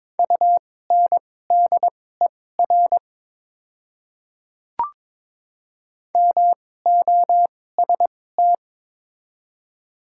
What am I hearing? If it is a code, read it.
Morse code, heard as a keyed tone, about 22 wpm, spaced as for 16 wpm, 700 Hz. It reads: UNDER MOST
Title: 2x Character Spacing - Top 200 Words - 22wpm